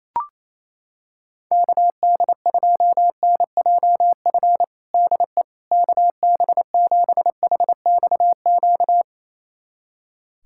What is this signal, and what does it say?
Morse code, heard as a keyed tone, about 27 wpm, 700 Hz. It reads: KD2NJF DE KB7HXQ
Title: USA Callsign DE Callsign - 6 Characters - 28wpm